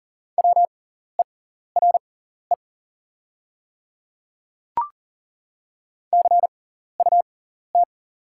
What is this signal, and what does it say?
Morse code, heard as a keyed tone, about 37 wpm, spaced as for 14 wpm, 700 Hz. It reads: WERE CUT
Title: 6x Character Spacing - Top 300 Words - 40wpm